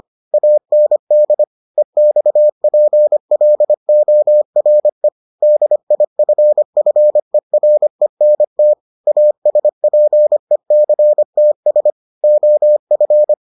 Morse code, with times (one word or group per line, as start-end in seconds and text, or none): 0.3-1.4 AND
1.8-5.1 EXPLORE
5.4-8.7 DIFFERENT
9.1-11.9 ASPECTS
12.2-13.3 OF